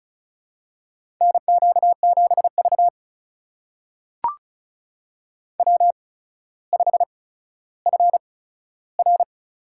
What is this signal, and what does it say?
Morse code, heard as a keyed tone, about 34 wpm, spaced as for 15 wpm, 700 Hz. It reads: NQ7V W5FR
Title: USA Call Signs - 4 Characters - 8x Character Spacing - 35wpm